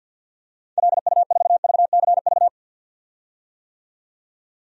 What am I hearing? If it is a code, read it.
Morse code, heard as a keyed tone, about 52 wpm, 700 Hz. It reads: LU4VCV
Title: DXCC-100 Call Signs - 6 Characters - 50wpm